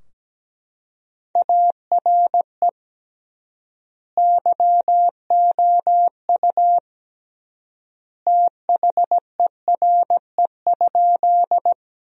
Morse code, heard as a keyed tone, about 17 wpm, 700 Hz. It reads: ARE YOU THERE?
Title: 3x Word-Spacing - Sentences from Top 100 Words - 17wpm